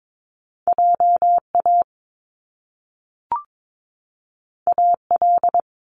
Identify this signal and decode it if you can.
Morse code, heard as a keyed tone, about 22 wpm, 700 Hz. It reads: JA AL
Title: Top Two Letter Combinations in Words 22wpm